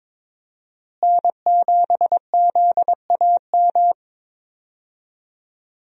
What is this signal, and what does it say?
Morse code, heard as a keyed tone, about 22 wpm, 700 Hz. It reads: N7ZAM
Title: USA Call Signs - 5 Characters - Repeated 3x - 22wpm